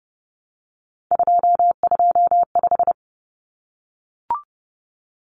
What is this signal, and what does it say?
Morse code, heard as a keyed tone, about 30 wpm, 700 Hz. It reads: 225
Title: Common Cut Numbers - 3-Digit 30wpm